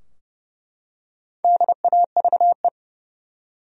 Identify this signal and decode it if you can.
Morse code, heard as a keyed tone, about 30 wpm, 700 Hz. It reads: DAVE
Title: USA Common Names - Max 5 Letters - 30wpm